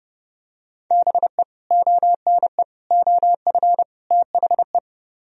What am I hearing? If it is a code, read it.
Morse code, heard as a keyed tone, about 31 wpm, 700 Hz. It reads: BE ONE OF THE